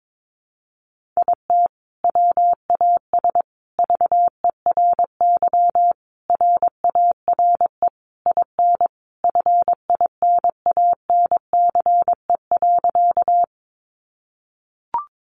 Morse code, heard as a keyed tone, about 22 wpm, 700 Hz. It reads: IT WAS VERY RARE IN FINANCE.
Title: Sentences from Top 1500 Words 22wpm